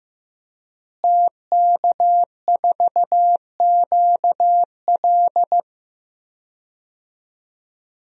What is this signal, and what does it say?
Morse code, heard as a keyed tone, about 15 wpm, 700 Hz. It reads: TK4QL
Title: DXCC-100 Call Signs - 5 Characters - Repeated 3x - 15wpm